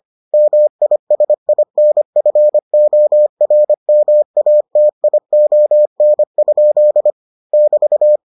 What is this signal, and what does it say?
Morse code, heard as a keyed tone, about 25 wpm, 600 Hz. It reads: MISINFORMATION? <BT>